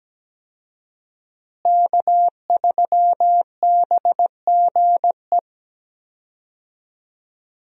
Morse code, heard as a keyed tone, about 17 wpm, 700 Hz. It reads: K3BGE
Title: USA Call Signs - 5 Characters - Repeated 2x - 17wpm